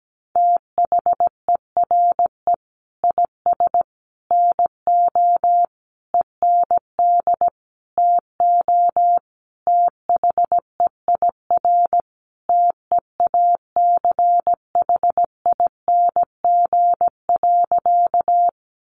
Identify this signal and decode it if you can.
Morse code, heard as a keyed tone, about 17 wpm, 700 Hz. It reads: THERE IS NO END TO THEIR TEACHING.
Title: Sentences from Top 1100 Words 17wpm